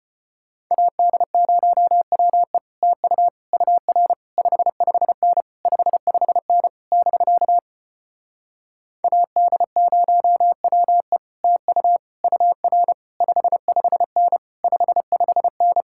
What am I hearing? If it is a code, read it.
Morse code, heard as a keyed tone, about 33 wpm, 700 Hz. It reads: AD0WE TU UR 55N 55N <BK> AD0WE TU UR 55N 55N